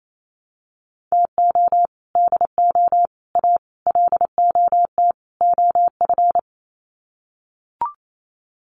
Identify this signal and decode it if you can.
Morse code, heard as a keyed tone, about 28 wpm, 690 Hz. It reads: TO DO A LOT OF
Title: Sets of 5 Words Encore 28wpm